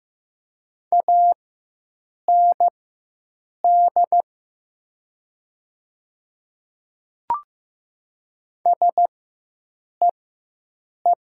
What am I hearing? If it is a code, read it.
Morse code, heard as a keyed tone, about 15 wpm, spaced as for 7 wpm, 700 Hz. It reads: AND SEE
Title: 4x Character Spacing - Top 100 Words - 15wpm